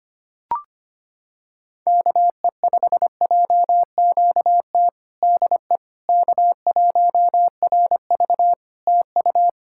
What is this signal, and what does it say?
Morse code, heard as a keyed tone, about 25 wpm, 700 Hz. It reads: KE5JQT DE K1RV TU
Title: Practice QSO - Exchange 4 - Ending - 25wpm